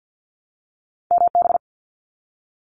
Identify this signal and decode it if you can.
Morse code, heard as a keyed tone, about 50 wpm, 700 Hz. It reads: MB
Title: Canadian Provinces and Territories 50wpm